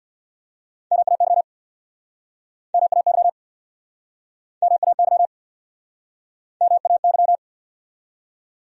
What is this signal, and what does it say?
Morse code, heard as a keyed tone, about 68 wpm, 700 Hz. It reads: MAY MAY MAY MAY